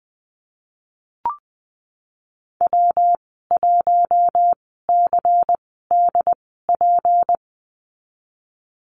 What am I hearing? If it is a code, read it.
Morse code, heard as a keyed tone, about 20 wpm, spaced as for 15 wpm, 700 Hz. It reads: W1CDP
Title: 154 Lesson 25g - Letter p - Call Signs 20wpm